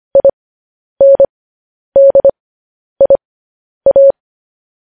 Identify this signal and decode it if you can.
Morse code, heard as a keyed tone, about 25 wpm, spaced as for 10 wpm, 550 Hz. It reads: INDIA
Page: news headlines in morse code at 10 wpm